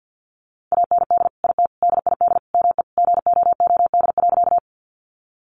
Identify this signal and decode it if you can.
Morse code, heard as a keyed tone, about 50 wpm, 700 Hz. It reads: AND IT DID ME GOOD.
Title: Sentences from Top 200 Words 50wpm